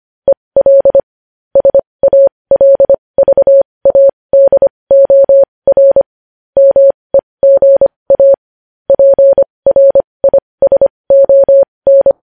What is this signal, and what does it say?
Morse code, heard as a keyed tone, about 25 wpm, spaced as for 20 wpm, 550 Hz. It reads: EL SALVADOR MEGA PRISON
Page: news-headlines-in-morse-code-at-20-wpm